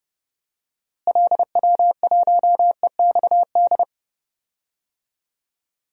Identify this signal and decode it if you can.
Morse code, heard as a keyed tone, about 32 wpm, 700 Hz. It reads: LW1EXD
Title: DXCC-100 Call Signs - 6 Characters - Repeated 2x - 30wpm